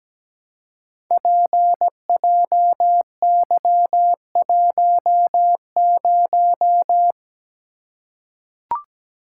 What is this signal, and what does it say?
Morse code, heard as a keyed tone, about 17 wpm, 700 Hz. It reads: PJY10
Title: Code Groups - 5 Letter:Numbers - 17wpm